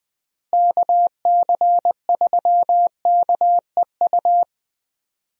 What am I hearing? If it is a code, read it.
Morse code, heard as a keyed tone, about 20 wpm, 700 Hz. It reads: KC3KEU
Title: USA Call Signs - 6 Characters - Repeated 3x - 20wpm